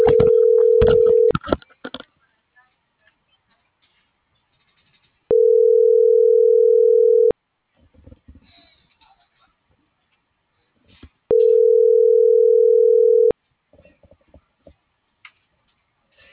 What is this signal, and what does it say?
Ringback tone (North America), 440 + 480 Hz.